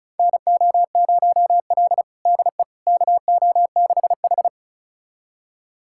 Morse code, heard as a keyed tone, about 36 wpm, 700 Hz. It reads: NO0L DE KO6H
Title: USA Callsign DE Callsign - 4 Characters - 35wpm